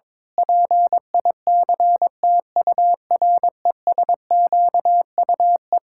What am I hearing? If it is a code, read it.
Morse code, heard as a keyed tone, about 21 wpm, 700 Hz. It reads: PICTURESQUE
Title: Pangram Challenge v2 - 22wpm